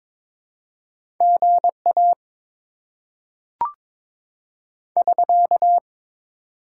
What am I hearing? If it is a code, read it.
Morse code, heard as a keyed tone, about 22 wpm, 700 Hz. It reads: GA <SK>
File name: Top 90 QSO Elements - 6x Character Spacing - 22wpm